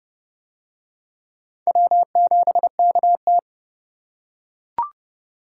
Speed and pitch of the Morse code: 29 wpm, 700 Hz

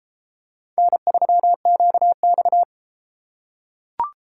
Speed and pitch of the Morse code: 31 wpm, 700 Hz

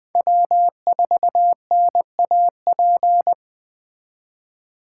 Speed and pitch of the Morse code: 21 wpm, 700 Hz

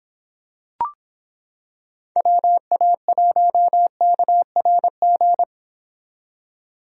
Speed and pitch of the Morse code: 26 wpm, 700 Hz

